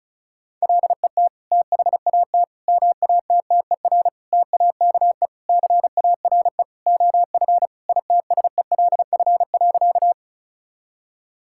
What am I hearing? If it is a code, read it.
Morse code, heard as a keyed tone, about 36 wpm, 700 Hz. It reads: LET THAT MATTER TAKE CARE OF ITSELF.
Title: Sentences from Top 1000 Words - Review Part 5 - 35wpm